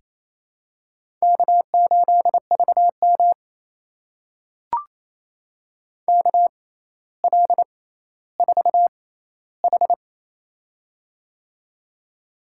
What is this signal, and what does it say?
Morse code, heard as a keyed tone, about 28 wpm, spaced as for 14 wpm, 700 Hz. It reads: K8VM KL4H
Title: USA Call Signs - 4 Characters - 6x Character Spacing - 28wpm